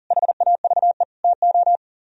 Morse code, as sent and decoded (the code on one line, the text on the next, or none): .... .- ...- . / - ---
HAVE TO